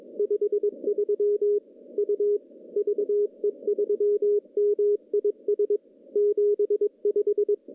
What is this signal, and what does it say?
Morse code, heard as a keyed tone, about 20 wpm, 410 Hz. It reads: 53 U VE3MIS 75
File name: contests / CW-SS-k5tr- -0200